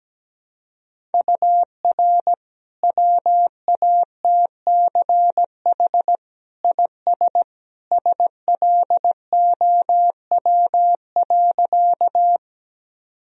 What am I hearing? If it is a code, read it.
Morse code, heard as a keyed tone, about 17 wpm, 700 Hz. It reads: UR WATCH IS SLOW.
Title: Sentences from Top 400 Words and Abbreviations 17wpm